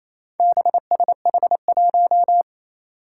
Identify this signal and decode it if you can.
Morse code, heard as a keyed tone, about 28 wpm, 700 Hz. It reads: BSH1
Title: Four Challenging Letter-Number Combinations 28wpm